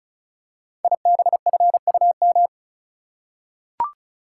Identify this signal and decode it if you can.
Morse code, heard as a keyed tone, about 36 wpm, 700 Hz. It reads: IBFUM
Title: Code Groups - 5 Letter:Numbers - 35wpm